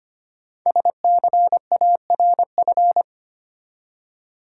Morse code, heard as a keyed tone, about 25 wpm, 690 Hz. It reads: SCARF
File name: practice / 25wpm / Easy Words with Hints - 25wpm